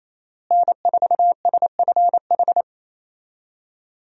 Morse code, heard as a keyed tone, about 28 wpm, 700 Hz. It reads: N4SFH
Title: USA Call Signs - 5 Characters - Repeated 2x - 28wpm